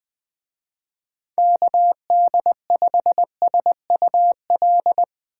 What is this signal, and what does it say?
Morse code, heard as a keyed tone, about 22 wpm, 700 Hz. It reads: KD5SUL
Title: Speed Racing - USA 2x3 Call Signs 20wpm